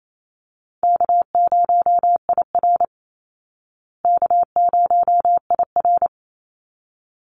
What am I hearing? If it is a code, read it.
Morse code, heard as a keyed tone, about 28 wpm, 700 Hz. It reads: K0IR K0IR
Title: Easy Call Signs Repeated 2x - 28wpm